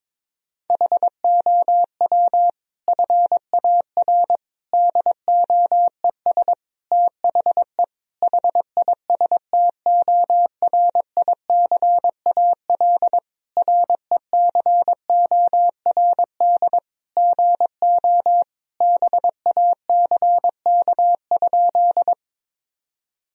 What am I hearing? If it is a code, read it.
Morse code, heard as a keyed tone, about 21 wpm, 700 Hz. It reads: HOW FAR DOES THE HISTORICAL RECORD GO BACK?
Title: Sentences from Top 1000 Words - Review Part 4 - 22wpm